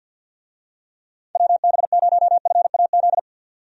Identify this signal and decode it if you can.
Morse code, heard as a keyed tone, about 48 wpm, 700 Hz. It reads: WB0FAZ
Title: USA Call Signs - 6 Characters - 50wpm